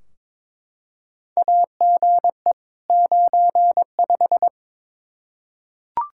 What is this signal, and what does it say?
Morse code, heard as a keyed tone, about 22 wpm, 700 Hz. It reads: AGE 95
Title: K1USN SST QSO Practice 22wpm